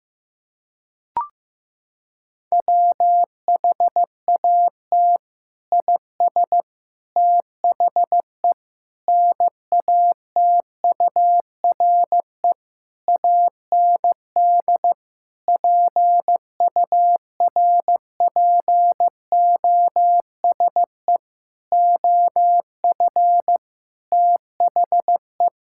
Morse code, heard as a keyed tone, about 16 wpm, 700 Hz. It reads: WHAT IS THE NATURE AND PURPOSE OF THE
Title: Sentences from Top 1000 Words - Review Part 5 - 15wpm